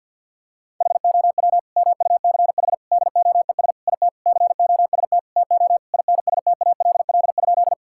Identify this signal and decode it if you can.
Morse code, heard as a keyed tone, about 50 wpm, 700 Hz. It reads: HOW MUCH DOES IT COST TO INSTALL?
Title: Sentences from Top 1600 Words 50wpm